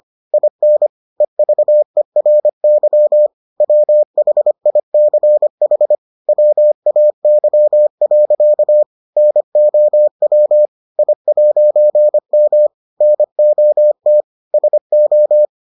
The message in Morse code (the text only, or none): IN EVERY WHICH WAY. NOW I'M NOT SO